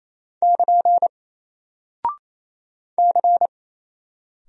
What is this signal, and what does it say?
Morse code, heard as a keyed tone, about 27 wpm, 700 Hz. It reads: <KN> C
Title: Single Character (Letter:Number:Punctuation:Prosign) 28wpm